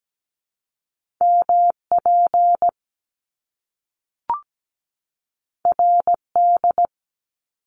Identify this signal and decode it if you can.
Morse code, heard as a keyed tone, about 17 wpm, 700 Hz. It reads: MP RD
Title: Top Two Letter Combinations in Words 17wpm